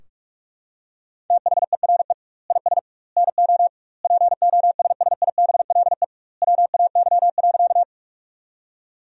Morse code, heard as a keyed tone, about 46 wpm, 700 Hz. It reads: THERE IS NO POSSIBLE WAY.
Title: 2x Word-Spacing - Sentences from Top 500 Words - 45wpm